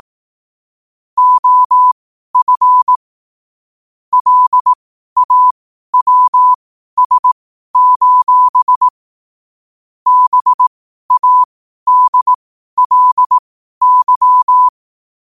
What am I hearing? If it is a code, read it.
Morse code, heard as a keyed tone, about 18 wpm, spaced as for 12 wpm, 1000 Hz. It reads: OF LAWS: BADLY
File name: qotdfiles / 12wpm